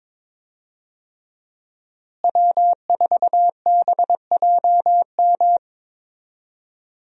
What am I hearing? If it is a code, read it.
Morse code, heard as a keyed tone, about 22 wpm, 700 Hz. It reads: W4BJM
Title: USA Call Signs - 5 Characters - Repeated 3x - 22wpm